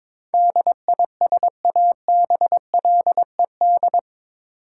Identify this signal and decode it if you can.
Morse code, heard as a keyed tone, about 22 wpm, 700 Hz. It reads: DISABLED
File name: practice / 22wpm / Eight Letter Words - 22wpm